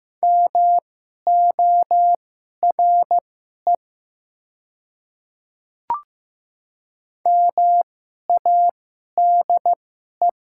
Morse code, heard as a keyed tone, about 15 wpm, spaced as for 11 wpm, 700 Hz. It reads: MORE MADE
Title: 2x Character Spacing - Top 200 Words - 15wpm